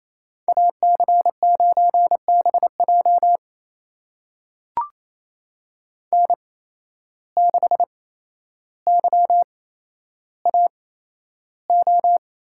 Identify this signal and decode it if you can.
Morse code, heard as a keyed tone, about 28 wpm, spaced as for 12 wpm, 700 Hz. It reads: AC9BJ N6YAO
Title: USA Call Signs - 5 Characters - 8x Character Spacing - 28wpm